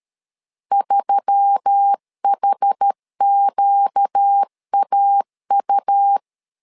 Morse code, heard as a keyed tone, about 13 wpm, 800 Hz. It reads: 3HQAU
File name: LEZIONI / 5 LBVC / 5 LB 12wpm 1906